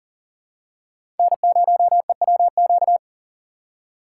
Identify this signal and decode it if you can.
Morse code, heard as a keyed tone, about 40 wpm, 700 Hz. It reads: N0EWQ
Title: USA Call Signs - 6x Character Spacing - 40wpm